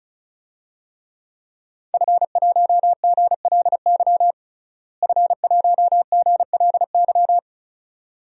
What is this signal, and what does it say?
Morse code, heard as a keyed tone, about 36 wpm, 700 Hz. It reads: F1GLY F1GLY
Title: DXCC-100 Call Signs - 5 Characters - Repeated 2x - 35wpm